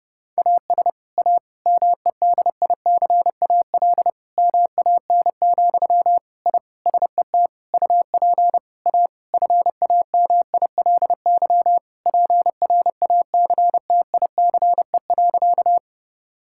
Morse code, heard as a keyed tone, about 30 wpm, 700 Hz. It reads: AS A MEDICAL MAN, I SET UP A FAMILY PRACTICE.
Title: Sentences from Top 1000 Words - Review Part 5 - 30wpm